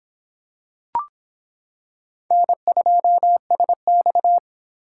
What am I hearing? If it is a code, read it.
Morse code, heard as a keyed tone, about 26 wpm, 700 Hz. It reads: N2SX